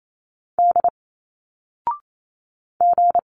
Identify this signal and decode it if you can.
Morse code, heard as a keyed tone, about 28 wpm, 700 Hz. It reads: D G
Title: Single Letters 28wpm